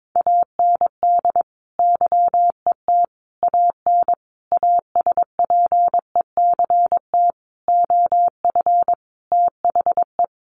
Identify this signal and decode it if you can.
Morse code, heard as a keyed tone, about 22 wpm, 700 Hz. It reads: AND YET AN ASPECT OF THE
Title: Sentences from Top 1300 Words 22wpm